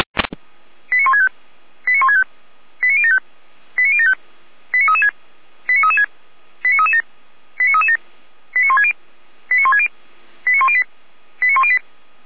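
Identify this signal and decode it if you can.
Five-tone selective call (ZVEI), heard as quick runs of five tones, tones of about 70 ms. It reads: 89165 89165 89075 89075 89336 89336 89337 89337 89160 89160 89108 89108
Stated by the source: Five-tone ZVEI selective call received by SDR.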